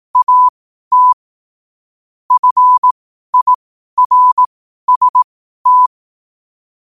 Morse code, heard as a keyed tone, about 18 wpm, spaced as for 12 wpm, 1000 Hz. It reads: AT FIRST
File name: qotdfiles / 12wpm